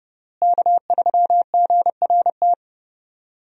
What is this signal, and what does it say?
Morse code, heard as a keyed tone, about 32 wpm, 700 Hz. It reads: K3GRT
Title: USA Call Signs - 5 Characters - Repeated 3x - 30wpm